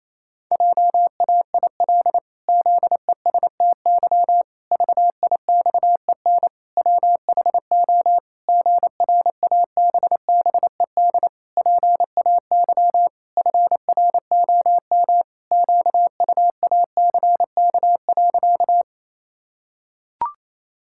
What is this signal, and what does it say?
Morse code, heard as a keyed tone, about 28 wpm, 700 Hz. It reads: JAIL ZESTY VIXEN WHO GRABBED PAY FROM QUACK.
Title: Pangram Challenge v2 - 28wpm